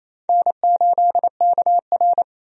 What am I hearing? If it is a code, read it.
Morse code, heard as a keyed tone, about 28 wpm, 700 Hz. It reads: N8KR
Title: USA Call Signs - 4 Characters - 28wpm